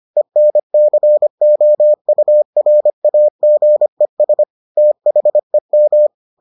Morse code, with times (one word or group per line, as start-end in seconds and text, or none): 0.2-4.4 ENCOURAGES
4.8-6.1 THEM